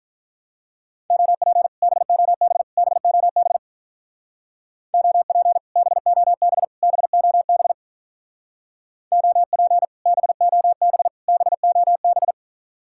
Morse code, heard as a keyed tone, about 45 wpm, 700 Hz. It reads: OP BOB BOB OP BOB BOB OP BOB BOB